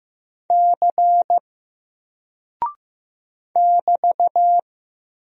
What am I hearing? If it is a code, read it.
Morse code, heard as a keyed tone, about 15 wpm, 700 Hz. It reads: C <BT>
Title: Single Character (Letter:Number:Punctuation:Prosign) Alt 15wpm